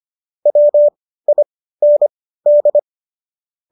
Morse code, heard as a keyed tone, about 25 wpm, spaced as for 15 wpm, 590 Hz. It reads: WIND